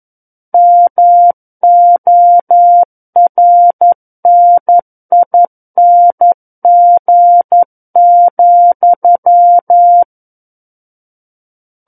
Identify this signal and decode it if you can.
Morse code, heard as a keyed tone, about 11 wpm, 700 Hz. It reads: MORNING,
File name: Book One / 02-Chapter 2 - 11 WPM